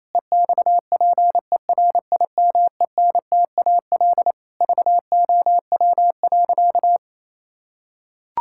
Morse code, heard as a keyed tone, about 28 wpm, 700 Hz. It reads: EXPERIMENTAL VOW.
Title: Pangram Challenge v2 - 28wpm